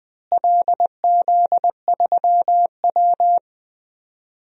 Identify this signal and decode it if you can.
Morse code, heard as a keyed tone, about 21 wpm, 700 Hz. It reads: LZ3W
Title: DXCC-100 Call Signs - 4 Characters - Repeated 3x - 20wpm